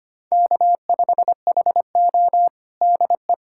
Morse code, heard as a keyed tone, about 25 wpm, 700 Hz. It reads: K5HO DE
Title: USA Callsign DE Callsign - 4 Characters - 25wpm